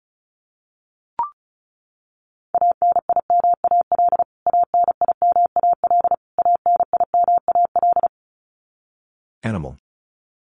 Morse code, heard as a keyed tone, about 34 wpm, 700 Hz. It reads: ANIMAL ANIMAL ANIMAL
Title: Top 500 Words Repeated 35wpm